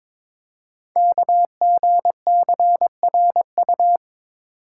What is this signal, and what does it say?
Morse code, heard as a keyed tone, about 22 wpm, 700 Hz. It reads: KGCRU
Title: Code Groups - 5 Letters - 22wpm